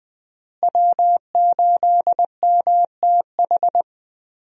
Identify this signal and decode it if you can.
Morse code, heard as a keyed tone, about 21 wpm, 700 Hz. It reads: W8MTH
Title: USA Call Signs - 5 Characters - Repeated 2x - 20wpm